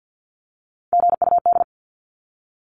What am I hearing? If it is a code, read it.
Morse code, heard as a keyed tone, about 48 wpm, 700 Hz. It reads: GUD
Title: Abbreviations 50wpm